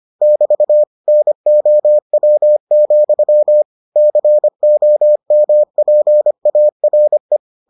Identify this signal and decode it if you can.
Morse code, heard as a keyed tone, about 25 wpm, 600 Hz. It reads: <BT> NOW, COMPARE